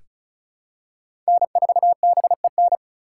Morse code, heard as a keyed tone, about 34 wpm, 700 Hz. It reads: N4BEN
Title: USA Call Signs - 5 Characters - 3x Character Spacing - 35wpm